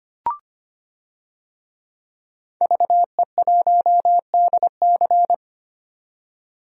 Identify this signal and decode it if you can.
Morse code, heard as a keyed tone, about 25 wpm, 700 Hz. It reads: VE1DC
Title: DXCC-100 Call Signs - 5 Characters - 25wpm